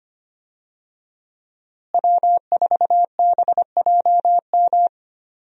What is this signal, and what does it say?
Morse code, heard as a keyed tone, about 25 wpm, 700 Hz. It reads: W4BJM
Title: USA Call Signs - 5 Characters - Repeated 3x - 25wpm